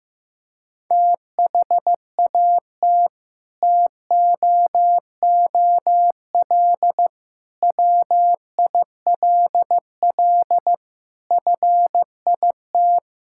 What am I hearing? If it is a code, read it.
Morse code, heard as a keyed tone, about 16 wpm, 700 Hz. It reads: THAT TOOL WILL FIT
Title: Sentences from Top 1000 Words - Review Part 6 - 15wpm